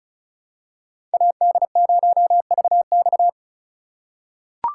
Morse code, heard as a keyed tone, about 36 wpm, 700 Hz. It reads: AD0VX
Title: USA Call Signs - 2x Character Spacing - 35wpm